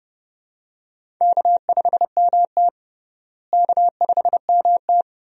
Morse code, heard as a keyed tone, about 30 wpm, 700 Hz. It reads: K5MT K5MT
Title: USA Call Signs - 4 Characters - Repeated 2x - 30wpm